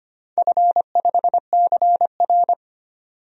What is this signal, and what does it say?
Morse code, heard as a keyed tone, about 25 wpm, 700 Hz. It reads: F5CR
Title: DXCC-100 Call Signs - 4 Characters - 25wpm